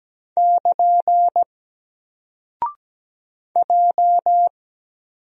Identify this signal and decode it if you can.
Morse code, heard as a keyed tone, about 17 wpm, 700 Hz. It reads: <KN> J
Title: Single Character (Letter:Number:Punctuation:Prosign) 17wpm